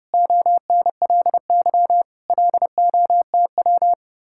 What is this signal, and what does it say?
Morse code, heard as a keyed tone, about 30 wpm, 700 Hz. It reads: ONLY LOTW
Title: Speed-Racing Common QSO Phrases - 25wpm